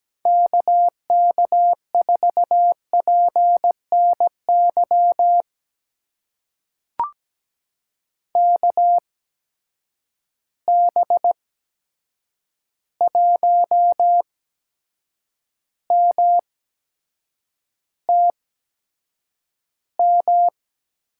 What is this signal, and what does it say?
Morse code, heard as a keyed tone, about 17 wpm, spaced as for 7 wpm, 700 Hz. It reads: KK4PNY KB1MTM
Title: USA Call Signs - 6 Characters - 8x Character Spacing - 17wpm